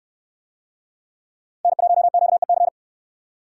Morse code, heard as a keyed tone, about 69 wpm, 700 Hz. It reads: N1QEC